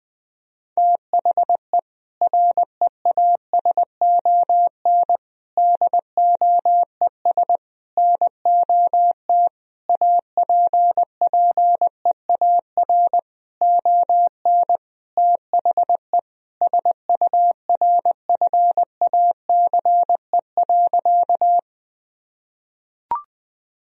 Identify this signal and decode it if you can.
Morse code, heard as a keyed tone, about 20 wpm, 700 Hz. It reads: THE REASON DOES NOT APPEAR ON THE SURFACE.
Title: Sentences from Top 1000 Words - Review Part 3 - 20wpm